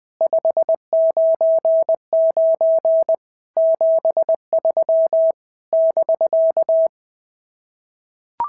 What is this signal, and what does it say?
Morse code, heard as a keyed tone, about 18 wpm, 650 Hz. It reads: 599 73 <BK>
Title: SOTA QSO Practice 20wpm